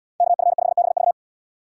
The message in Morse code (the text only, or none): KF5BV